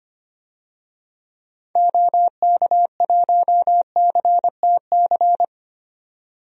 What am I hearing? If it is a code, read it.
Morse code, heard as a keyed tone, about 25 wpm, 700 Hz. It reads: OK1CTC
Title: DXCC-100 Call Signs - 6 Characters - 25wpm